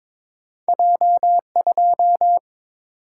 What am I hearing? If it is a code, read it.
Morse code, heard as a keyed tone, about 23 wpm, 700 Hz. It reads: J 2
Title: Single-Letter Single-Number 22wpm